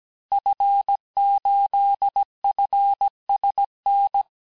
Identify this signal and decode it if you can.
Morse code, heard as a keyed tone, about 17 wpm, 800 Hz. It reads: F8FSN
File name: corrige-cours-1581-fin-du-QSO-vitesse-16-mpm